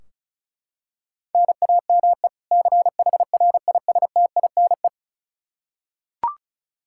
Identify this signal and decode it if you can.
Morse code, heard as a keyed tone, about 36 wpm, 700 Hz. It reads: NAME CHRISTINE
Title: Practice QSO - Exchange 1 - Beginning - 35wpm